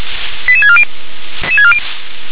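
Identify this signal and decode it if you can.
Five-tone selective call (ZVEI), heard as quick runs of five tones, tones of about 70 ms. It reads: 88640
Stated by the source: ZVEI five-tone selective call received on an SDR.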